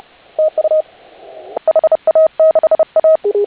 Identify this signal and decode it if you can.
Morse code, heard as a keyed tone, about 30 wpm, 630 Hz. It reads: TU HA6A
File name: HA6A CW